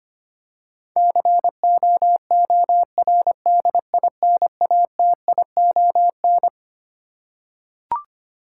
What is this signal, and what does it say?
Morse code, heard as a keyed tone, about 25 wpm, 700 Hz. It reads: COORDINATION
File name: Twelve Letter Words - 25wpm